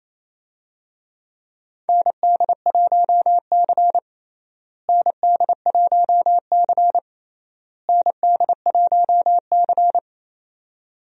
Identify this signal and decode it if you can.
Morse code, heard as a keyed tone, about 28 wpm, 700 Hz. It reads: ND1C ND1C ND1C
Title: USA Call Signs - 4 Characters - Repeated 3x - 28wpm